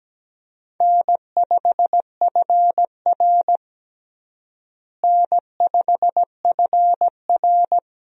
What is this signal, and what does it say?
Morse code, heard as a keyed tone, about 17 wpm, 700 Hz. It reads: N5FR N5FR
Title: USA Call Signs - 4 Characters - Repeated 2x - 17wpm